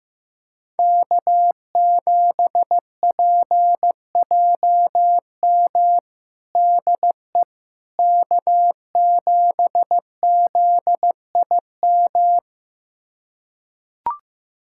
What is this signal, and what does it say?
Morse code, heard as a keyed tone, about 15 wpm, 700 Hz. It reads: K7PJM DE K7ZIM
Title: USA Callsign DE Callsign - 5 Characters - 15wpm